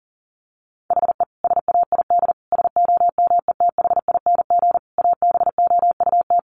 Morse code, heard as a keyed tone, about 40 wpm, 700 Hz. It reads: HE SAID SOMETHING ABOUT
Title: Sentences from Top 1500 Words 40wpm